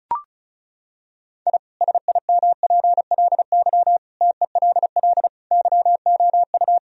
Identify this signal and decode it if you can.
Morse code, heard as a keyed tone, about 34 wpm, 700 Hz. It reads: I SIMPLY TELL YOU
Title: Sentences from Top 1000 Words - Review Part 4 - 35wpm